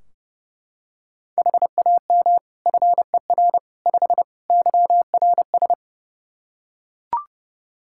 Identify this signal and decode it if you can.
Morse code, heard as a keyed tone, about 30 wpm, 700 Hz. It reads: HAM FER 5 YRS
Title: K1USN SST QSO Practice 30wpm